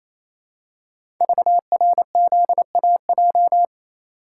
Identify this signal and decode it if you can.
Morse code, heard as a keyed tone, about 29 wpm, 700 Hz. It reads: VRZAJ
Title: Code Groups - 5 Letters - 28wpm